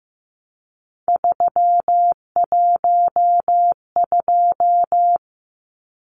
Morse code, heard as a keyed tone, about 15 wpm, 700 Hz. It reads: 312
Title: Numbers - 3-Digit 15wpm